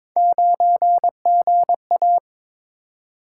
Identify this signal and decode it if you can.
Morse code, heard as a keyed tone, about 22 wpm, 700 Hz. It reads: 9GA
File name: practice / 22wpm / Three Letter-Number Combinations - 22wpm